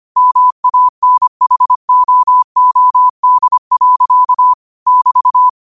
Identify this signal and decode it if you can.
Morse code, heard as a keyed tone, about 25 wpm, 1000 Hz. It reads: MANHOOD. <BT>
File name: qotdfiles / 25wpm